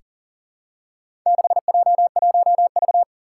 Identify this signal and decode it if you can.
Morse code, heard as a keyed tone, about 40 wpm, 700 Hz. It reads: 6J1V
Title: Four Challenging Letter-Number Combinations 40wpm